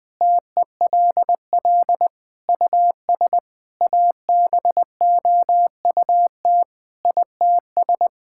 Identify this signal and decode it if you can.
Morse code, heard as a keyed tone, about 21 wpm, 700 Hz. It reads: TELL US ABOUT ITS
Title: Sentences from Top 1000 Words - Review Part 6 - 20wpm